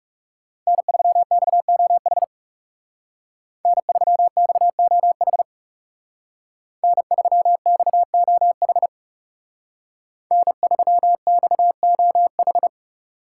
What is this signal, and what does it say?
Morse code, heard as a keyed tone, about 37 wpm, 700 Hz. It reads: N3XOH N3XOH N3XOH N3XOH
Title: Speed Racing - USA 1x3 Call Signs 30wpm